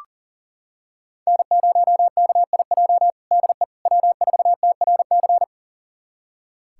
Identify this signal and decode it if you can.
Morse code, heard as a keyed tone, about 37 wpm, 700 Hz. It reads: N0KIJ DE W4TRC
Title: USA Callsign DE Callsign - 5 Characters - 40wpm